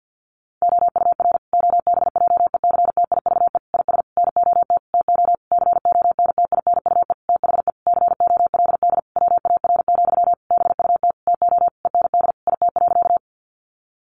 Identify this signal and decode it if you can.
Morse code, heard as a keyed tone, about 50 wpm, 700 Hz. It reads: OUR OBJECTIVE IS NOT TO CONTINUE THE COLD WAR, BUT TO END IT.
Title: Sentences from Top 1200 Words 50wpm